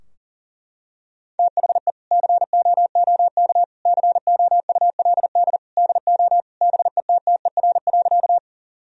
Morse code, heard as a keyed tone, about 37 wpm, 700 Hz. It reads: THE COOK COULD DO BETTER.
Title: Sentences from Top 1000 Words - Review Part 4 - 40wpm